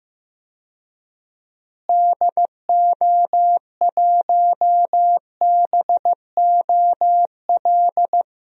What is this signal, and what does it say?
Morse code, heard as a keyed tone, about 15 wpm, 700 Hz. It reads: DO1BOL
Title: DXCC-100 Call Signs - 6 Characters - Repeated 2x - 15wpm